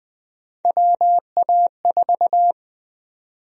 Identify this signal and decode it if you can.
Morse code, heard as a keyed tone, about 21 wpm, 700 Hz. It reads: WA4
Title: USA Call Sign Prefixes - 20wpm